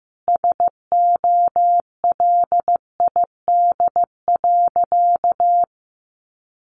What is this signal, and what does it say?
Morse code, heard as a keyed tone, about 15 wpm, 700 Hz. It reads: SOLID.
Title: Sentences from Top 1500 Words 15wpm